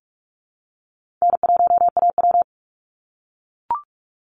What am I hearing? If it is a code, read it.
Morse code, heard as a keyed tone, about 44 wpm, 690 Hz. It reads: N1AW
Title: NA SOTA Chaser Callsigns 45wpm